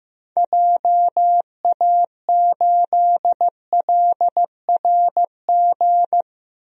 Morse code, heard as a keyed tone, about 15 wpm, 700 Hz. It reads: JA8LRG
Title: DXCC-100 Call Signs - 6 Characters - Repeated 3x - 15wpm